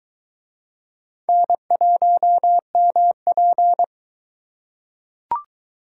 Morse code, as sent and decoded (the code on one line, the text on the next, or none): -. .---- -- .--.
N1MP